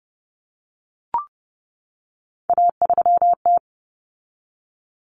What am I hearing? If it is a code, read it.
Morse code, heard as a keyed tone, about 30 wpm, 700 Hz. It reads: A3T
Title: Common Cut Numbers - 3-Digit 30wpm